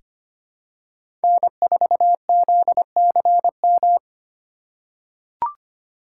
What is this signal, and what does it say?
Morse code, heard as a keyed tone, about 25 wpm, 700 Hz. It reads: N4ZCM
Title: USA Call Signs - 5 Characters - 3x Character Spacing - 25wpm